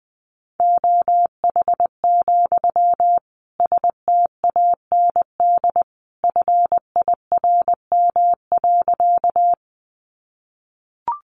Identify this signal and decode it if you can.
Morse code, heard as a keyed tone, about 20 wpm, 700 Hz. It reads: OH, STAND FIRM.
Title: Sentences from Top 1400 Words 20wpm